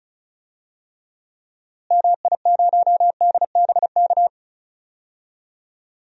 Morse code, heard as a keyed tone, about 36 wpm, 700 Hz. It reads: MI0DBK
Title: DXCC-100 Call Signs - 6 Characters - Repeated 2x - 35wpm